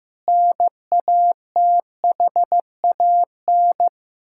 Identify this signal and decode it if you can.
Morse code, heard as a keyed tone, about 16 wpm, 700 Hz. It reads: NATHAN
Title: Practice QSO - Exchange 1 - Beginning - 15wpm